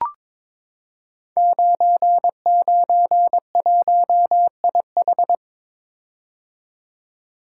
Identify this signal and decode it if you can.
Morse code, heard as a keyed tone, about 22 wpm, 700 Hz. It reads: 991IH
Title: Code Groups - 5 Letter:Numbers - 22wpm